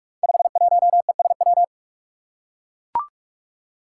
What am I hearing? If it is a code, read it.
Morse code, heard as a keyed tone, about 42 wpm, 700 Hz. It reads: 51ESW
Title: Five Challenging Letter-Number Combinations 45wpm